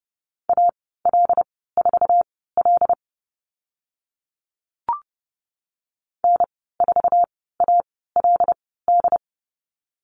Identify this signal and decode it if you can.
Morse code, heard as a keyed tone, about 30 wpm, spaced as for 17 wpm, 700 Hz. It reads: AL4L N4ALD
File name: Lesson 14g - Number 2 - Call Signs 30wpm